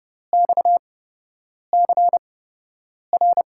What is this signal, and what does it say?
Morse code, heard as a keyed tone, about 32 wpm, spaced as for 8 wpm, 700 Hz. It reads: XCR
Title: USA Call Signs - 5 Characters - 8x Character Spacing - 30wpm